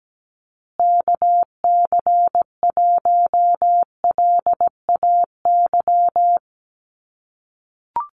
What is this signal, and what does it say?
Morse code, heard as a keyed tone, about 17 wpm, 700 Hz. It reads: KC1LAY
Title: USA Call Signs Repeated 2x 17wpm